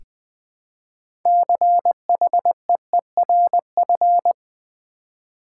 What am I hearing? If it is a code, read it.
Morse code, heard as a keyed tone, about 20 wpm, 700 Hz. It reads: CHEERF
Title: Build A Word - 2 Letters At A Time - 8-Letter Words - 20wpm